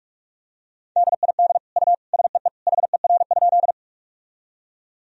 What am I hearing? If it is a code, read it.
Morse code, heard as a keyed tone, about 46 wpm, 700 Hz. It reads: DID U SEE HER?
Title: Sentences from Top 100 Words and Abbreviations 45wpm